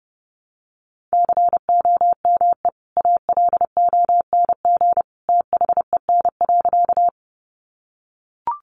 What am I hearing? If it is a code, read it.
Morse code, heard as a keyed tone, about 30 wpm, 700 Hz. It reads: COME ALONG THEN.
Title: Sentences from Top 700 Words 30wpm